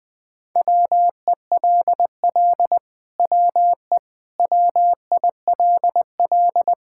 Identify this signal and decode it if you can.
Morse code, heard as a keyed tone, about 20 wpm, 700 Hz. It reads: WELL WE WILL
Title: Sentences with 7-Word Phrases - Spoken Beginning - 20wpm